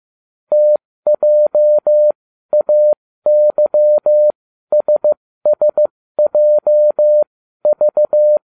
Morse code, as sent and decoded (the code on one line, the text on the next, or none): - .--- .- -.-- ... ... .--- ...-
TJAYSSJV